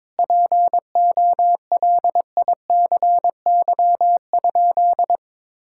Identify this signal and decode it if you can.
Morse code, heard as a keyed tone, about 22 wpm, 700 Hz. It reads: POLICY?
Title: Sentences from Top 1000 Words - Review Part 5 - 22wpm